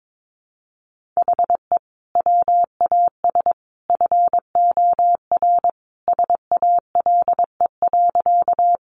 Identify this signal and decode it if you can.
Morse code, heard as a keyed tone, about 22 wpm, 700 Hz. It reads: HE WAS FOR SALE.
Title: Sentences from Top 1400 Words 22wpm